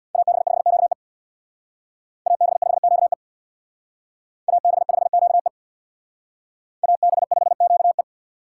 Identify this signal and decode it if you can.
Morse code, heard as a keyed tone, about 61 wpm, 690 Hz. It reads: AB5QE AB5QE AB5QE AB5QE
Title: Speed Racing - USA 2x2 Call Signs 50wpm